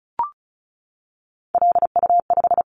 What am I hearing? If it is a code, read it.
Morse code, heard as a keyed tone, about 36 wpm, 700 Hz. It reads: LU5
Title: Callsign Prefixes for Top 50 DXCC - 35wpm